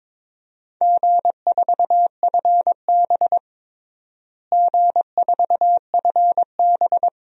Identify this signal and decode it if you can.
Morse code, heard as a keyed tone, about 22 wpm, 700 Hz. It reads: G4FB G4FB
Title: DXCC-100 Call Signs - 4 Characters - Repeated 2x - 22wpm